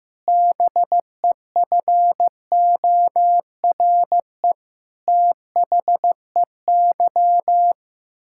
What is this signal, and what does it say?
Morse code, heard as a keyed tone, about 16 wpm, 700 Hz. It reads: BEFORE THEY
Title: Pangram Challenge v2 - 15wpm